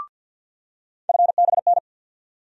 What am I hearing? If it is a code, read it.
Morse code, heard as a keyed tone, about 52 wpm, 700 Hz. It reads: FBN
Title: USA Call Sign Postfixes - 3 Letters 50wpm